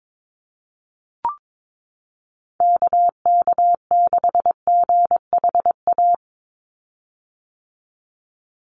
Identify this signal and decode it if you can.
Morse code, heard as a keyed tone, about 22 wpm, 700 Hz. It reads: KK6GHA
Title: Call Signs - 22wpm